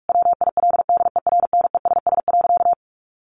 Morse code, heard as a keyed tone, about 46 wpm, 700 Hz. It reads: WILDERNESS.
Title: A Sound Of Thunder 45wpm